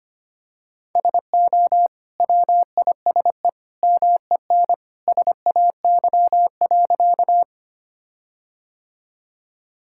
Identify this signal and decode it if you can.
Morse code, heard as a keyed tone, about 25 wpm, 700 Hz. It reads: SO WISE MEN SAY.
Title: Sentences from Top 1600 Words 25wpm